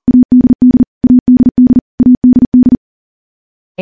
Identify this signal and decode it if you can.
Morse code, heard as a keyed tone, about 40 wpm, 250 Hz. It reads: ADD ADD ADD
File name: morse / morse-training-40WPM